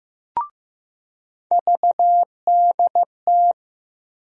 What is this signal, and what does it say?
Morse code, heard as a keyed tone, about 15 wpm, 700 Hz. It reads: VDT
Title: USA Call Sign Postfixes - 3 Letters 15wpm